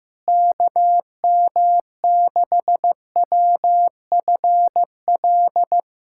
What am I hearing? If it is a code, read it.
Morse code, heard as a keyed tone, about 16 wpm, 700 Hz. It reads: KM6WFL
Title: USA Callsign DE Callsign - 6 Characters - 15wpm